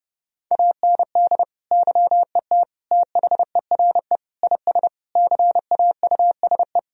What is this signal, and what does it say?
Morse code, heard as a keyed tone, about 30 wpm, 700 Hz. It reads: AND YET THERE IS CAUSE
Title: Sentences from Top 1000 Words - Review Part 1 - 30wpm